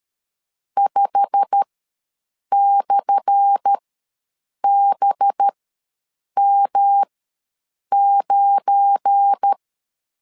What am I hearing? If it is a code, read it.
Morse code, heard as a keyed tone, about 13 wpm, spaced as for 7 wpm, 800 Hz. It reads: 5/BM9
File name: LEZIONI / 10 SIGNS / 10A SLASH ws 12wpm 3443